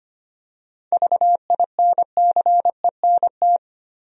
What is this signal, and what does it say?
Morse code, heard as a keyed tone, about 25 wpm, 700 Hz. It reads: VINCENT
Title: USA Common Names - 25wpm